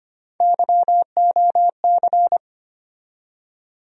Morse code, heard as a keyed tone, about 25 wpm, 700 Hz. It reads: YOC
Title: USA Call Sign Postfixes - 3 Letters 25wpm